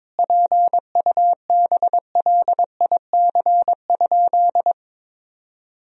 Morse code, heard as a keyed tone, about 22 wpm, 700 Hz. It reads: PUBLIC?
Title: Sentences from Top 1000 Words - Review Part 3 - 22wpm